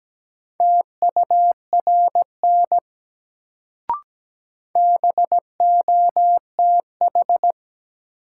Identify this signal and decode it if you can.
Morse code, heard as a keyed tone, about 17 wpm, 700 Hz. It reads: TURN BOTH
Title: Sets of 4 Words - New Word Drill - 17wpm